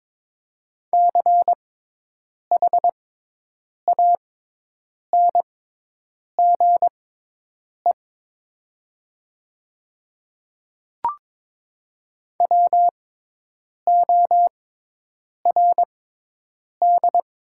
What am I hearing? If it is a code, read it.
Morse code, heard as a keyed tone, about 21 wpm, spaced as for 8 wpm, 700 Hz. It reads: CHANGE WORD
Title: 6x Character Spacing - Top 300 Words - 22wpm